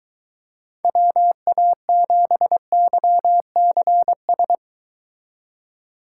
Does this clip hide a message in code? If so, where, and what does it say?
Yes: Morse code from 0.8 to 4.5 s, heard as a keyed tone, about 23 wpm, 700 Hz. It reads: WA7YCS